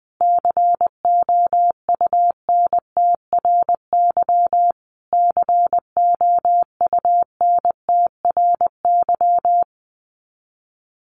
Morse code, heard as a keyed tone, about 20 wpm, 700 Hz. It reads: COUNTRY COUNTRY
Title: Top 500 Words Repeated 20wpm